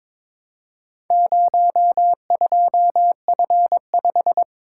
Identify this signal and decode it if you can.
Morse code, heard as a keyed tone, about 22 wpm, 700 Hz. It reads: 02F5
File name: practice / 22wpm / Four Letter-Number Combinations - 22wpm